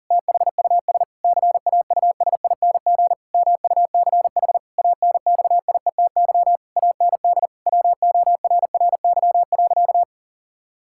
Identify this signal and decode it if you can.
Morse code, heard as a keyed tone, about 40 wpm, 700 Hz. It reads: THUS CAUSING MUCH ANXIETY AND WORRY.
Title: Pangram Challenge v2 - 40wpm